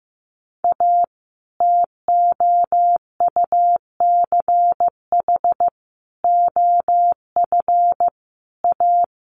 Morse code, heard as a keyed tone, about 16 wpm, 700 Hz. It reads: A TOUCH OF A
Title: A Sound Of Thunder 15wpm